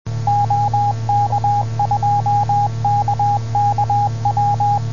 Morse code, heard as a keyed tone, about 21 wpm, 820 Hz, with radio noise. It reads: OK2KKW